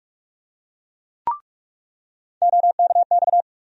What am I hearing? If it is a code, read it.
Morse code, heard as a keyed tone, about 46 wpm, 700 Hz. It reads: OKX
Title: USA Call Sign Postfixes - 3 Letters 45wpm